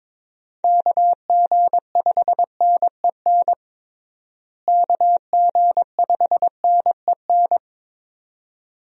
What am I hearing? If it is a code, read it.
Morse code, heard as a keyed tone, about 21 wpm, 700 Hz. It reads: KG5NEN KG5NEN
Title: USA Call Signs - 6 Characters - Repeated 2x - 22wpm